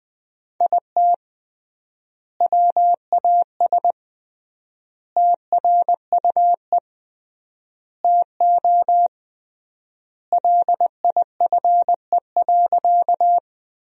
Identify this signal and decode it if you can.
Morse code, heard as a keyed tone, about 20 wpm, 700 Hz. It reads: IT WAS TRUE TO LIFE.
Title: 3x Word-Spacing - Sentences from Top 400 Words - 20wpm